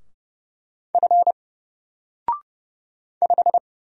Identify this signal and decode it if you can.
Morse code, heard as a keyed tone, about 30 wpm, 700 Hz. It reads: F 5
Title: Single Letter-Number Alt 30wpm